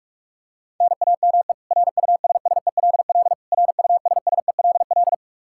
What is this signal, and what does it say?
Morse code, heard as a keyed tone, about 41 wpm, 700 Hz. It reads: NAME RUSSELL RUSSELL
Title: K1USN SST QSO Practice 45wpm